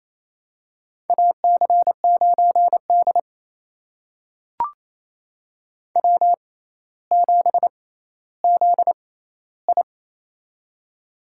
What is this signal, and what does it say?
Morse code, heard as a keyed tone, about 28 wpm, spaced as for 14 wpm, 700 Hz. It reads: AC9D W7ZI
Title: USA Call Signs - 4 Characters - 6x Character Spacing - 28wpm